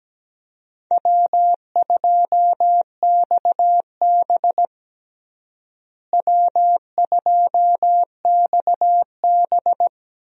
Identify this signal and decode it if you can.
Morse code, heard as a keyed tone, about 17 wpm, 700 Hz. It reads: W2XB W2XB
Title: USA Call Signs - 4 Characters - Repeated 3x - 17wpm